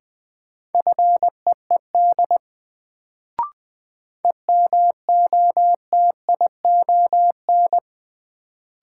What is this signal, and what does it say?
Morse code, heard as a keyed tone, about 20 wpm, 700 Hz. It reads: FEED EMOTION
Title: Sentences from Top 1600 Words - New Word Drill - 20wpm